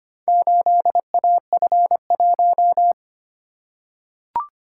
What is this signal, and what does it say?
Morse code, heard as a keyed tone, about 25 wpm, 700 Hz. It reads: 8AF1
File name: practice / 25wpm / Four Letter-Number Combinations - 25wpm